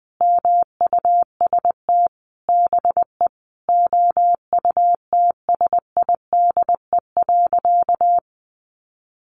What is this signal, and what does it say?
Morse code, heard as a keyed tone, about 20 wpm, 700 Hz. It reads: MUST BE OUTSIDE.
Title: Sentences from Top 1200 Words 20wpm